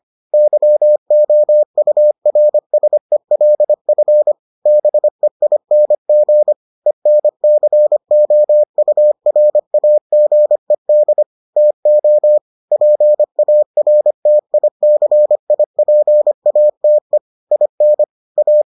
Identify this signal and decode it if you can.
Morse code, heard as a keyed tone, about 25 wpm, 600 Hz. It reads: YOURSELF BEING ENCOURAGED TO PARTICIPATE IN A